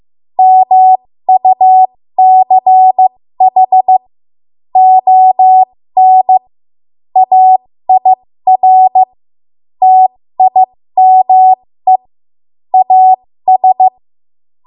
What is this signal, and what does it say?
Morse code, heard as a keyed tone, about 15 wpm, spaced as for 13 wpm, 750 Hz. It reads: MUCH ON AIR TIME AS
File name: Archive / 13 WPM / 13WPM